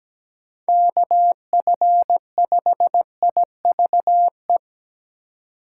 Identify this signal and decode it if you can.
Morse code, heard as a keyed tone, about 17 wpm, 700 Hz. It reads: KF5IVE
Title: USA Call Signs - 6 Characters - 4x Character Spacing - 17wpm